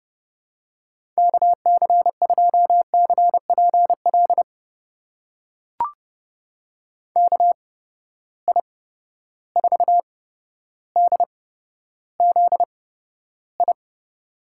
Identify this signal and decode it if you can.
Morse code, heard as a keyed tone, about 30 wpm, spaced as for 13 wpm, 700 Hz. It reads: KC2CPL KI4DZI
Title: USA Call Signs - 8x Character Spacing - 30wpm